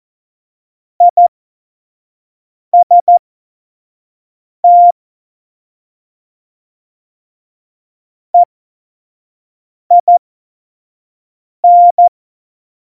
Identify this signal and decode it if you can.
Morse code, heard as a keyed tone, about 14 wpm, spaced as for 5 wpm, 700 Hz. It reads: IST EIN
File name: Morsekurs / Witze / Witz 20. November 2018, 25 F70